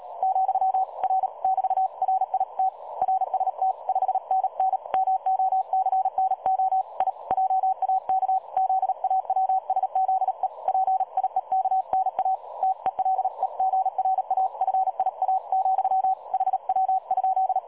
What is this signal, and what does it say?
Morse code, heard as a keyed tone, about 36 wpm, 760 Hz. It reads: , R <BT>RIT RST 5NN MOCNO I JAK ZAWSZE PIEKNA TELEGRAFIA, HW?
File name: dlazaawansowanych / nagraniazpasma / websdr recording 3565.0kHz